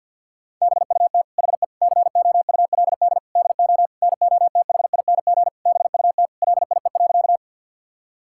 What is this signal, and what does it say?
Morse code, heard as a keyed tone, about 50 wpm, 700 Hz. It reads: BUT HE COULD DO NOTHING BUT LIE.
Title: Sentences from Top 1600 Words 50wpm